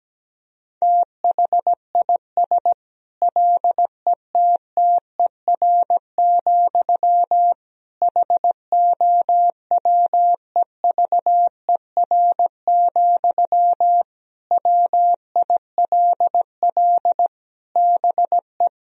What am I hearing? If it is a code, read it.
Morse code, heard as a keyed tone, about 17 wpm, 700 Hz. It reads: THIS LETTER, HOWEVER, WILL BE
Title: Sentences from Top 1000 Words - Review Part 4 - 17wpm